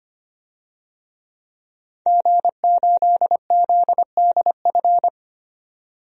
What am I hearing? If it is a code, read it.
Morse code, heard as a keyed tone, about 25 wpm, 700 Hz. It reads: G8ZDF
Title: DXCC-100 Call Signs - 5 Characters - 25wpm